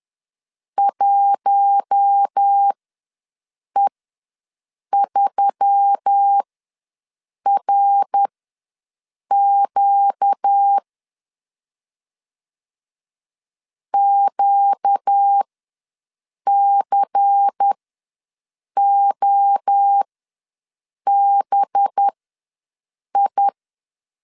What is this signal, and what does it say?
Morse code, heard as a keyed tone, about 10 wpm, spaced as for 6 wpm, 800 Hz. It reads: 1E3RQ QCOBI
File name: LEZIONI / 5 LBVC / 5 LBVC ws 10wpm 3733